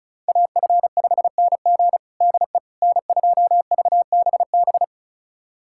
Morse code, heard as a keyed tone, about 36 wpm, 700 Hz. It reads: AF5NG DE N2VBB
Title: USA Callsign DE Callsign - 5 Characters - 35wpm